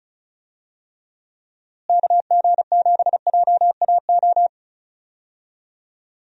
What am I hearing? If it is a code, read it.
Morse code, heard as a keyed tone, about 34 wpm, 700 Hz. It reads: KG7JAO
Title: USA Call Signs - 6 Characters - Repeated 3x - 35wpm